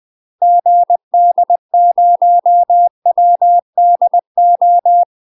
Morse code, heard as a keyed tone, about 20 wpm, 700 Hz. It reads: GD0WDO
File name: UK MASTER QSOs TRAINING / 20WPM / UK MASTER QSOs 0020